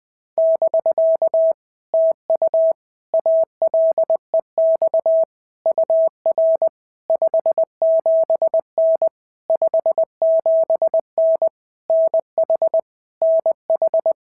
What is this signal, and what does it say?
Morse code, heard as a keyed tone, about 19 wpm, 650 Hz. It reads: <BK> TU ALEX UR 57N 57N NH NH
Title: POTA QSO Practice - 20wpm